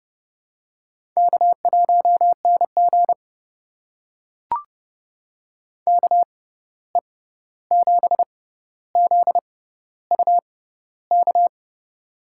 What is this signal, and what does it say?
Morse code, heard as a keyed tone, about 31 wpm, spaced as for 14 wpm, 700 Hz. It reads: K1NG KE7ZUK
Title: USA Call Signs - 6x Character Spacing - 30wpm